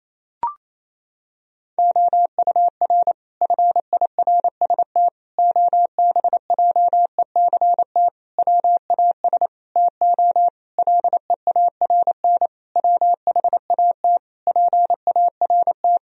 Morse code, heard as a keyed tone, about 28 wpm, 700 Hz. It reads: OUR FIRST OBJECT WAS TO LEARN WHAT PART
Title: Sentences from Top 1000 Words - Review Part 3 - 28wpm